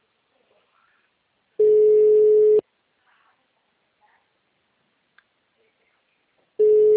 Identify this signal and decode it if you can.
Ringback tone (Europe, 425 Hz).